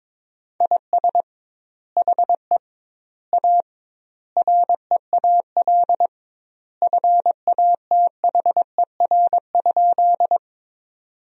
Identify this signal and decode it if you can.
Morse code, heard as a keyed tone, about 22 wpm, 700 Hz. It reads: IS HE A REAL FATHER?
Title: 2x Word-Spacing - Sentences from Top 300 Words - 22wpm